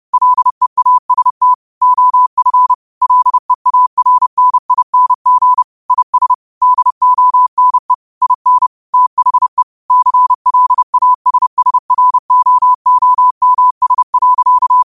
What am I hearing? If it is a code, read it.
Morse code, heard as a keyed tone, about 30 wpm, 1000 Hz. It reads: LEAST OF LEARNING IS DONE IN THE CLASSROOMS.
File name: qotdfiles / 30wpm